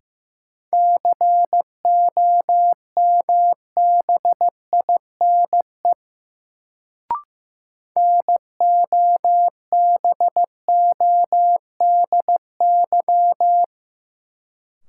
Morse code, heard as a keyed tone, about 15 wpm, 700 Hz. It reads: COMBINE NOBODY
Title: Sentences from Top 1600 Words - New Word Drill - 15wpm